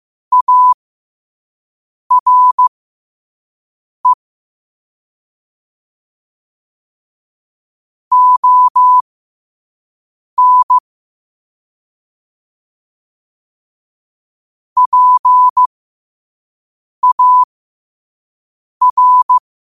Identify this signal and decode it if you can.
Morse code, heard as a keyed tone, about 15 wpm, spaced as for 5 wpm, 1000 Hz. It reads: ARE ON PAR
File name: qotdfiles / 5wpm